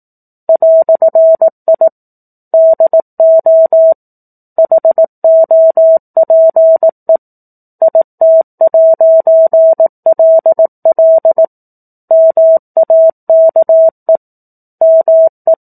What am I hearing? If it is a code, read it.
Morse code, heard as a keyed tone, about 18 wpm, 650 Hz. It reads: "I DO HOPE IT'LL MAKE ME
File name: 18 WPM